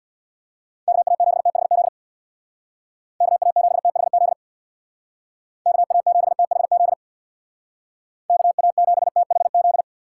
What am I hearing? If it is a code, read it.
Morse code, heard as a keyed tone, about 65 wpm, 700 Hz. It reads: KA7THZ KA7THZ KA7THZ KA7THZ